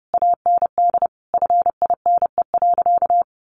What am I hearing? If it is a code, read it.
Morse code, heard as a keyed tone, about 30 wpm, 700 Hz. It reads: AND FINE.
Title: Sentences from Top 1300 Words 30wpm